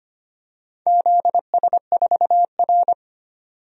Morse code, heard as a keyed tone, about 25 wpm, 700 Hz. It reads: ZS4R
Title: DXCC-100 Call Signs - 4 Characters - 25wpm